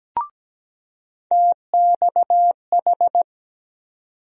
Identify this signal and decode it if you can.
Morse code, heard as a keyed tone, about 17 wpm, 700 Hz. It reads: TXH
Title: USA Call Sign Postfixes - 3 Letters 17wpm